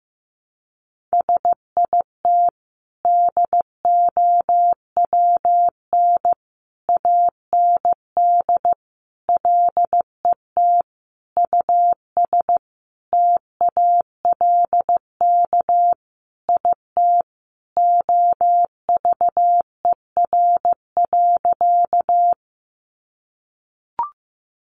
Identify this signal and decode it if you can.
Morse code, heard as a keyed tone, about 15 wpm, 700 Hz. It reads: SIT DOWN AND LET US TALK IT OVER.
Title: Sentences from Top 500 Words 15wpm